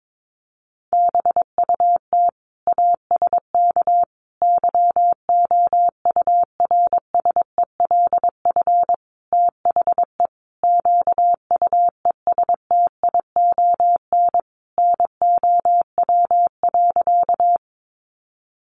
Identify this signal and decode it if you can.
Morse code, heard as a keyed tone, about 22 wpm, 700 Hz. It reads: BUT ASK YOURSELF THE QUESTION NOW.
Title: Sentences from Top 600 Words 22wpm